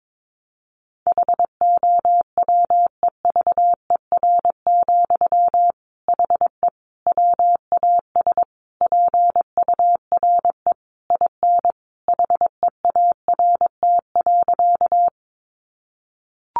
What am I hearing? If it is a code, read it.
Morse code, heard as a keyed tone, about 22 wpm, 700 Hz. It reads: HOWEVER, HE WAS PURE IN HEART.
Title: Sentences from Top 1500 Words 22wpm